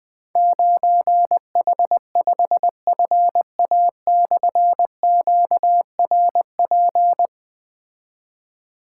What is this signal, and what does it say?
Morse code, heard as a keyed tone, about 20 wpm, 700 Hz. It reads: 9H5FA/QRP
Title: Difficult Call Signs 20wpm